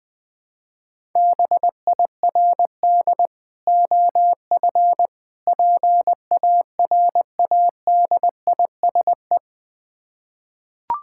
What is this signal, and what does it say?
Morse code, heard as a keyed tone, about 20 wpm, 700 Hz. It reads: BIRD OF PARADISE
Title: Difficult Words with Hints - 20wpm